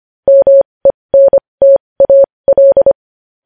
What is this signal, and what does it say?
Morse code, heard as a keyed tone, about 25 wpm, spaced as for 20 wpm, 550 Hz. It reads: MENTAL
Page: news-headlines-in-morse-code-at-20-wpm